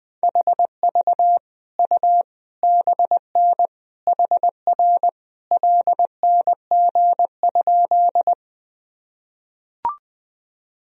Morse code, heard as a keyed tone, about 20 wpm, 700 Hz. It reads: HV U BN HR LNG?
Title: Sentences from Top 200 Words and Abbreviations 20wpm